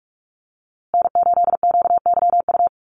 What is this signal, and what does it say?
Morse code, heard as a keyed tone, about 44 wpm, 700 Hz. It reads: N8QYU
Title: USA Call Signs Repeated 2x 45wpm